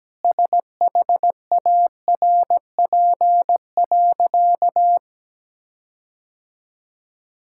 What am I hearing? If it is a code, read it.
Morse code, heard as a keyed tone, about 17 wpm, 700 Hz. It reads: SHARP.
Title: Sentences from Top 1600 Words 17wpm